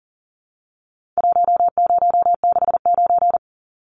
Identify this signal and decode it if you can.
Morse code, heard as a keyed tone, about 40 wpm, 700 Hz. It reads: 1069
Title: Numbers - Contest Style - 4-Digit 40wpm